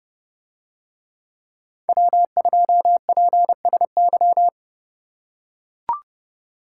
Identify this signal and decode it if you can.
Morse code, heard as a keyed tone, about 31 wpm, 700 Hz. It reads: W2PSY